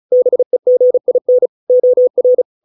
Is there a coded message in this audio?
Yes: Morse code from 0.1 to 2.4 s, heard as a keyed tone, about 34 wpm, 500 Hz. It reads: BEGIN OR